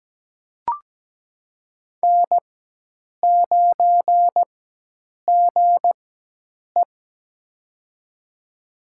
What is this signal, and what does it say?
Morse code, heard as a keyed tone, about 17 wpm, spaced as for 8 wpm, 700 Hz. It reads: N9GE
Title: USA Call Signs - 4x Character Spacing - 17wpm